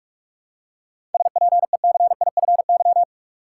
Morse code, heard as a keyed tone, about 46 wpm, 700 Hz. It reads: SPECIFY
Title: Seven Letter Words - 45wpm